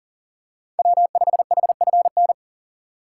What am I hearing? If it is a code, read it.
Morse code, heard as a keyed tone, about 37 wpm, 700 Hz. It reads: W5HFN
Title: USA Call Signs - 5 Characters - 4x Character Spacing - 40wpm